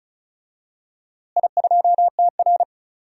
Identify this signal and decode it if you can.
Morse code, heard as a keyed tone, about 36 wpm, 700 Hz. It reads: I2TR
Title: DXCC-100 Call Signs - 4 Characters - Repeated 3x - 35wpm